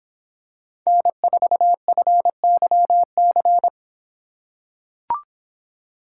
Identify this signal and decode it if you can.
Morse code, heard as a keyed tone, about 23 wpm, 700 Hz. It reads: N4FYC